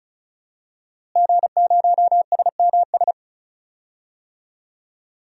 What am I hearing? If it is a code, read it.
Morse code, heard as a keyed tone, about 34 wpm, 700 Hz. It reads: G0SMS
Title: DXCC-100 Call Signs - 5 Characters - Repeated 2x - 35wpm